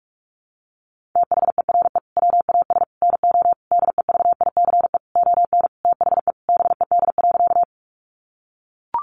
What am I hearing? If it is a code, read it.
Morse code, heard as a keyed tone, about 44 wpm, 700 Hz. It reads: THERE WAS NO DEVICE ON THE BED.
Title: Sentences from Top 1500 Words 45wpm